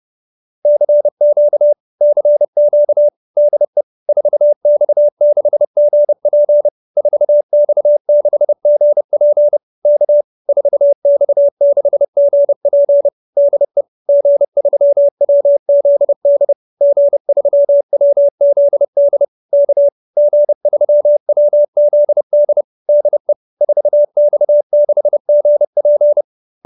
Morse code, heard as a keyed tone, about 30 wpm, 590 Hz. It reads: CQ CQ DE 4X6GP 4X6GP K 4X6GP DE G3WZD G3WZD K G3WZD DE 4X6GP
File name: QSO310 30